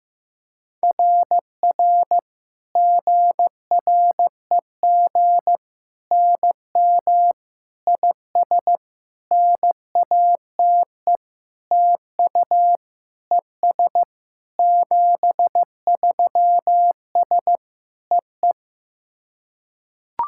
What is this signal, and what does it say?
Morse code, heard as a keyed tone, about 15 wpm, 700 Hz. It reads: RR GREG NM IS NATE TU ES 73S EE
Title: SOTA QSO Practice 15wpm